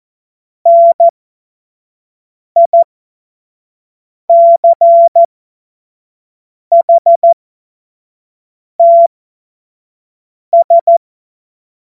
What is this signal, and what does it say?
Morse code, heard as a keyed tone, about 14 wpm, spaced as for 5 wpm, 680 Hz. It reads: NICHTS
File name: Morsekurs / Witze / Witz 27 Februar 2024 25 F70